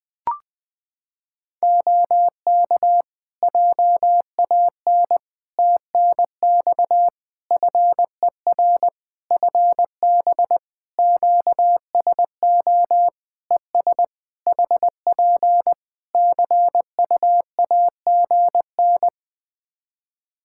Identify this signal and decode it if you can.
Morse code, heard as a keyed tone, about 20 wpm, 700 Hz. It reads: OK JAN TNX FER FB QSO ES HP CUAGN
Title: K1USN SST QSO Practice 20wpm